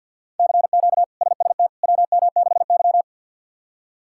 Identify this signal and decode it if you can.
Morse code, heard as a keyed tone, about 50 wpm, 700 Hz. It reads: CQ SST WM6Y
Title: K1USN SST QSO Practice - 50wpm